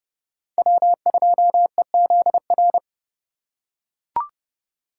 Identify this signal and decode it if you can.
Morse code, heard as a keyed tone, about 30 wpm, 700 Hz. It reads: W2EZR
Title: USA Call Signs - 5 Characters - 4x Character Spacing - 30wpm